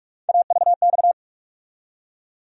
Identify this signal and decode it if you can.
Morse code, heard as a keyed tone, about 45 wpm, 700 Hz. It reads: AVX
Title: USA Call Sign Postfixes - 3 Letters 45wpm